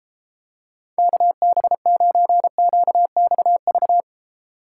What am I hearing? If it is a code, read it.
Morse code, heard as a keyed tone, about 31 wpm, 700 Hz. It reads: KB9QXV